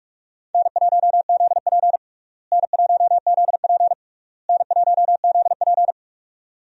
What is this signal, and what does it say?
Morse code, heard as a keyed tone, about 46 wpm, 700 Hz. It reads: N1ZP N1ZP N1ZP
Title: USA Call Signs - 4 Characters - Repeated 3x - 45wpm